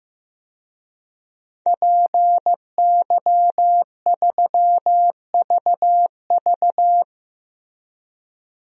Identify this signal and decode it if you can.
Morse code, heard as a keyed tone, about 16 wpm, 700 Hz. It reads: PY3VV
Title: DXCC-100 Call Signs - 5 Characters - Repeated 2x - 15wpm